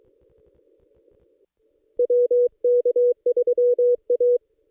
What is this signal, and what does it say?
Morse code, heard as a keyed tone, about 23 wpm, 490 Hz. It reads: WK3A
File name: contests / ss-cw-k5tr-op-n5rz -1900